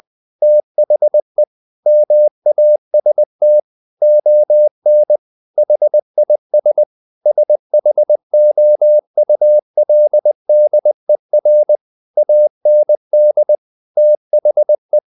Morse code, heard as a keyed tone, about 20 wpm, 600 Hz. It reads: THE MAST ON HIS SHOULDER AND THE
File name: the old man and the sea / the old man and the sea part 1 20wpm